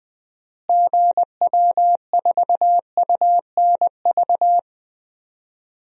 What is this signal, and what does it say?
Morse code, heard as a keyed tone, about 20 wpm, 700 Hz. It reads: GW4UNV
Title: DXCC-100 Call Signs - 6 Characters - Repeated 3x - 20wpm